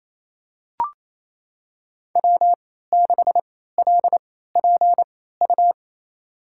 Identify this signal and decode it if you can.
Morse code, heard as a keyed tone, about 28 wpm, spaced as for 16 wpm, 700 Hz. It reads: W6LPU
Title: USA Call Signs - 3x Character Spacing - 28wpm